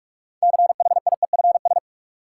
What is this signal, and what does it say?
Morse code, heard as a keyed tone, about 46 wpm, 700 Hz. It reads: CHIEFS
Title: Sets of 5 Words Encore - New Word Drill - 45wpm